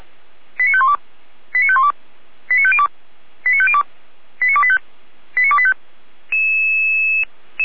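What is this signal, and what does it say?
Five-tone selective call (ZVEI), heard as quick runs of five tones, tones of about 70 ms. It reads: 89512 89512 89502 89502 89285 89285
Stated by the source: ZVEI five-tone call 89025 received on an SDR.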